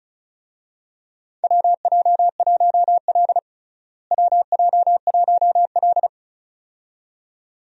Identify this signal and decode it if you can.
Morse code, heard as a keyed tone, about 36 wpm, 700 Hz. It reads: WJ1L WJ1L
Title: USA Call Signs - 4 Characters - Repeated 2x - 35wpm